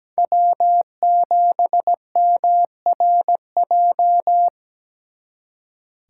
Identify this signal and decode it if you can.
Morse code, heard as a keyed tone, about 20 wpm, 700 Hz. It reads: W7MRJ
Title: Speed Racing - USA Call Signs 17wpm